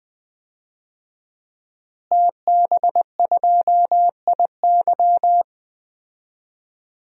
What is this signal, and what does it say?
Morse code, heard as a keyed tone, about 20 wpm, 700 Hz. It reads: TB2IY
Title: DXCC-100 Call Signs - 5 Characters - Repeated 2x - 20wpm